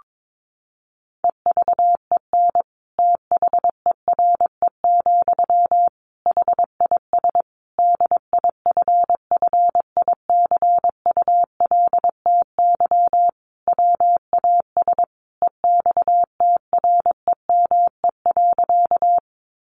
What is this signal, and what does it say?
Morse code, heard as a keyed tone, about 22 wpm, 700 Hz. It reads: EVEN THERE, HIS DIFFICULTY WAS EXTREME.
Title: Sentences from Top 1500 Words 22wpm